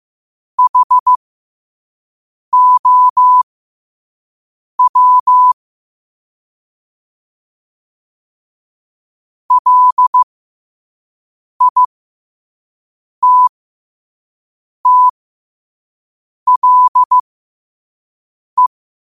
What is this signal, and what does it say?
Morse code, heard as a keyed tone, about 15 wpm, spaced as for 5 wpm, 1000 Hz. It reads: HOW LITTLE
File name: qotdfiles / 5wpm